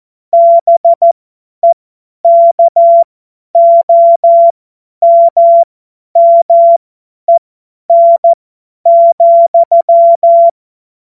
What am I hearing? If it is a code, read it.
Morse code, heard as a keyed tone, about 14 wpm, spaced as for 10 wpm, 680 Hz. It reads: BEKOMMEN,
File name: Morsekurs / Witze / Witz 21. Juni 2022 50 F70